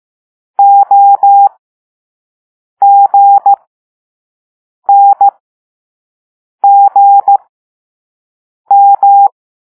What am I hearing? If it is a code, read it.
Morse code, heard as a keyed tone, about 15 wpm, spaced as for 5 wpm, 800 Hz. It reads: OGNGM